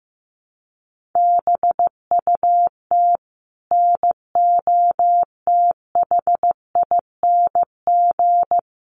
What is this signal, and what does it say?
Morse code, heard as a keyed tone, about 15 wpm, 700 Hz. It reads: BUT NOTHING
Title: Sentences from Top 1100 Words 15wpm